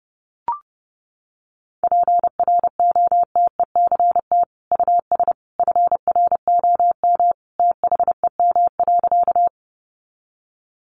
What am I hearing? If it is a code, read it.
Morse code, heard as a keyed tone, about 30 wpm, 700 Hz. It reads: PROTECT US FROM THEM.
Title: Sentences from Top 900 Words 30wpm